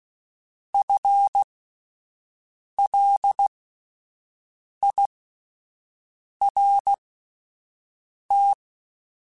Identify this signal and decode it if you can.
Morse code, heard as a keyed tone, about 16 wpm, spaced as for 5 wpm, 800 Hz. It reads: FLIRT
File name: Alphabetical / Alphabetical mp3s / Fs @5